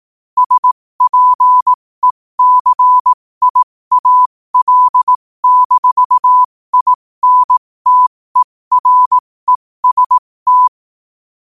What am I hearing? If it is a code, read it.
Morse code, heard as a keyed tone, about 18 wpm, spaced as for 15 wpm, 1000 Hz. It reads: SPECIAL-INTEREST
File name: qotdfiles / 15wpm